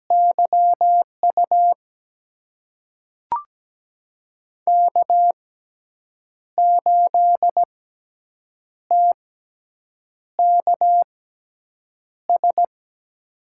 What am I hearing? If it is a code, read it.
Morse code, heard as a keyed tone, about 17 wpm, spaced as for 7 wpm, 700 Hz. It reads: YU K8TKS
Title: USA Call Signs - 5 Characters - 6x Character Spacing - 17wpm